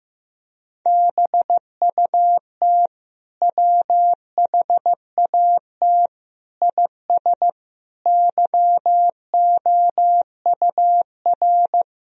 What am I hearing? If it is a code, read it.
Morse code, heard as a keyed tone, about 15 wpm, 700 Hz. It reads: BUT WHAT IS YOUR
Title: Sentences from Top 1000 Words - Review Part 3 - 15wpm